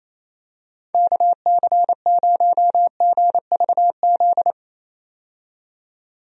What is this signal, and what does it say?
Morse code, heard as a keyed tone, about 27 wpm, 700 Hz. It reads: KC0GVZ
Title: USA Call Signs - 6 Characters - 28wpm